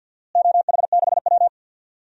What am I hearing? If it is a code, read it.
Morse code, heard as a keyed tone, about 48 wpm, 700 Hz. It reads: OH6W
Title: DXCC-100 Call Signs - 4 Characters - Repeated 3x - 50wpm